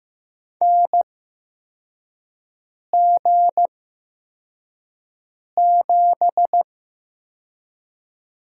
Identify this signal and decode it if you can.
Morse code, heard as a keyed tone, about 15 wpm, spaced as for 4 wpm, 700 Hz. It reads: NG7
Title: USA Call Signs - 4 Characters - 8x Character Spacing - 15wpm